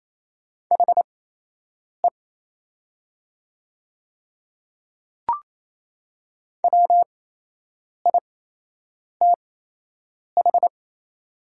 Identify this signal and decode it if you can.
Morse code, heard as a keyed tone, about 28 wpm, spaced as for 8 wpm, 700 Hz. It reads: HE WITH
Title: 8x Character Spacing - Top 100 Words - 28wpm